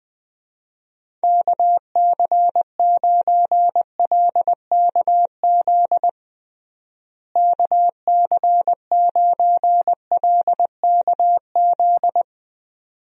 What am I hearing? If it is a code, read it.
Morse code, heard as a keyed tone, about 21 wpm, 700 Hz. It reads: KC9LKZ KC9LKZ
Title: USA Call Signs - 6 Characters - Repeated 2x - 20wpm